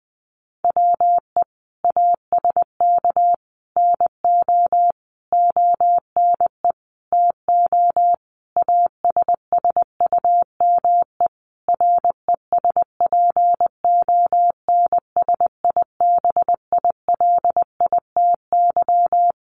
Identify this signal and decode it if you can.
Morse code, heard as a keyed tone, about 20 wpm, 700 Hz. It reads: WE ASK NO ONE TO ASSUME RESPONSIBILITY
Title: Sentences from Top 1500 Words 20wpm